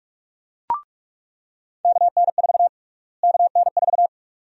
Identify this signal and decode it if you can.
Morse code, heard as a keyed tone, about 46 wpm, 700 Hz. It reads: KN4 KN4
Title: USA Call Sign Prefixes - Repeated 2x - 45wpm